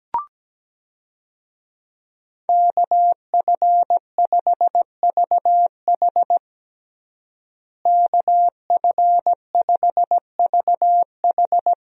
Morse code, heard as a keyed tone, about 17 wpm, 700 Hz. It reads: KF5VH KF5VH
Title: USA Call Signs - 5 Characters - Repeated 3x - 17wpm